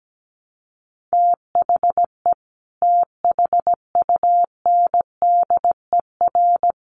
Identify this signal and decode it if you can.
Morse code, heard as a keyed tone, about 17 wpm, 700 Hz. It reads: THE THUNDER
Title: A Sound Of Thunder 17wpm